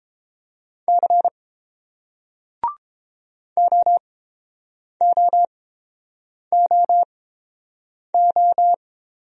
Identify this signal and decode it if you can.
Morse code, heard as a keyed tone, about 30 wpm, spaced as for 7 wpm, 700 Hz. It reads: C OOOO